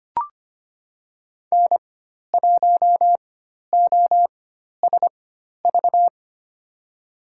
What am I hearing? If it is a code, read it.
Morse code, heard as a keyed tone, about 25 wpm, spaced as for 12 wpm, 700 Hz. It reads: N1OSV
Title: USA Call Signs - 4x Character Spacing - 25wpm